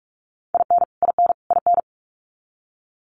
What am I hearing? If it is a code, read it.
Morse code, heard as a keyed tone, about 44 wpm, 700 Hz. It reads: IN IN IN
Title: Top 300 Words Repeated 45wpm